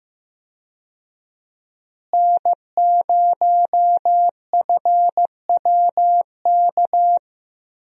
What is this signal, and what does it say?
Morse code, heard as a keyed tone, about 15 wpm, 700 Hz. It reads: N0FWK
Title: USA Call Signs - 5 Characters - Repeated 2x - 15wpm